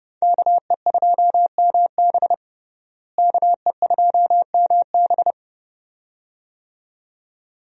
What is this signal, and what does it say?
Morse code, heard as a keyed tone, about 30 wpm, 700 Hz. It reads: KE2MB KE2MB
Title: USA Call Signs - 5 Characters - Repeated 2x - 30wpm